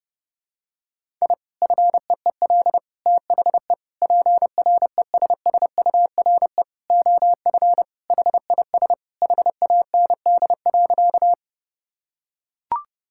Morse code, heard as a keyed tone, about 30 wpm, 700 Hz. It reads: I FEEL THE PRESSURE OF HIS HAND.
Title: Sentences from Top 1000 Words - Review Part 6 - 30wpm